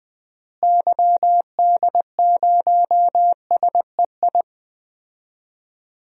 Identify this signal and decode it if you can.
Morse code, heard as a keyed tone, about 20 wpm, 700 Hz. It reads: YD0SEI
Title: DXCC-100 Call Signs - 6 Characters - Repeated 2x - 20wpm